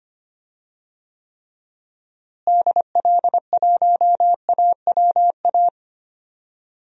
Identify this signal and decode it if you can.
Morse code, heard as a keyed tone, about 25 wpm, 700 Hz. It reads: DL1AWA
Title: DXCC-100 Call Signs - 6 Characters - Repeated 3x - 25wpm